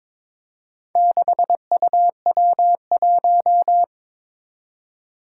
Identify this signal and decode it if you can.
Morse code, heard as a keyed tone, about 22 wpm, 700 Hz. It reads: 6UW1
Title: Four Challenging Letter-Number Combinations 22wpm